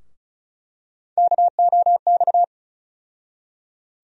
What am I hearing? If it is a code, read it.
Morse code, heard as a keyed tone, about 36 wpm, 700 Hz. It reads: KOX
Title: USA Call Sign Postfixes - 3 Letters 35wpm